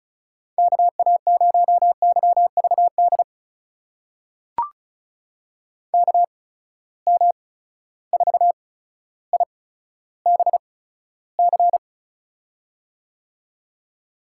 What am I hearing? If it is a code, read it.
Morse code, heard as a keyed tone, about 36 wpm, spaced as for 15 wpm, 700 Hz. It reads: KA0YVD KM4IBC
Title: USA Call Signs - 6 Characters - 8x Character Spacing - 35wpm